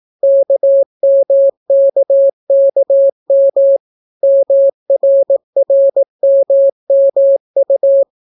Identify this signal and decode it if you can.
Morse code, heard as a keyed tone, about 18 wpm, 550 Hz. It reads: KMKKM MRRMMU